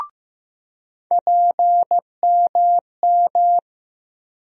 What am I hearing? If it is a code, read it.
Morse code, heard as a keyed tone, about 15 wpm, 700 Hz. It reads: PMM
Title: USA Call Sign Postfixes - 3 Letters 15wpm